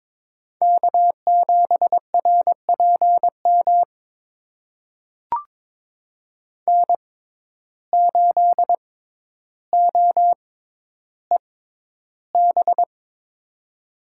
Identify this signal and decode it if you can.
Morse code, heard as a keyed tone, about 23 wpm, spaced as for 11 wpm, 700 Hz. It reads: K7RPM N8OEB
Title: USA Call Signs - 5 Characters - 6x Character Spacing - 22wpm